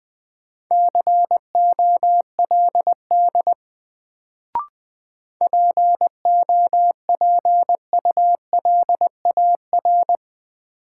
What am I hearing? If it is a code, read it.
Morse code, heard as a keyed tone, about 20 wpm, 700 Hz. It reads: COLD POPULAR
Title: Sets of 5 Words - New Word Drill - 20wpm